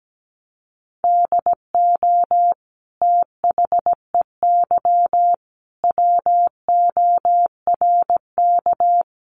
Morse code, heard as a keyed tone, about 17 wpm, 700 Hz. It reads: DO THEY WORK
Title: Sentences from Top 1300 Words 17wpm